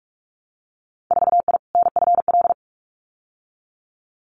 Morse code, heard as a keyed tone, about 46 wpm, 700 Hz. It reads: 4I NFL
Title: Winter Field Day Exchanges Encore 45wpm